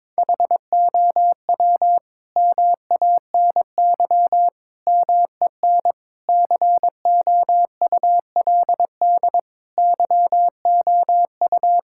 Morse code, heard as a keyed tone, about 22 wpm, 700 Hz. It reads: HOW MANY MEN COULD YOU
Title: Sentences from Top 1600 Words 22wpm